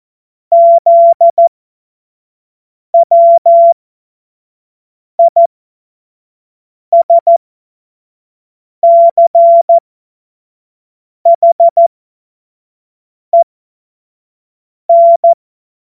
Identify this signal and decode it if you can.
Morse code, heard as a keyed tone, about 14 wpm, spaced as for 5 wpm, 680 Hz. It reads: ZWISCHEN
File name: Morsekurs / Witze / Witz 29 April 2025 25 F70